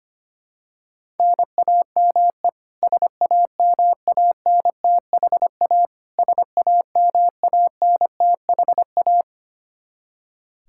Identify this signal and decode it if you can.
Morse code, heard as a keyed tone, about 25 wpm, 700 Hz. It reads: NAME SAMANTHA SAMANTHA
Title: Practice QSO - Exchange 1 - Beginning - 25wpm